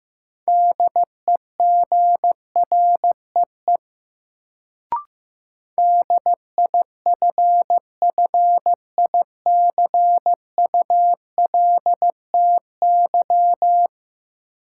Morse code, heard as a keyed tone, about 15 wpm, 700 Hz. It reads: DEGREE DIFFICULTY
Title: Sentences from Top 1500 Words - New Word Drill - 15wpm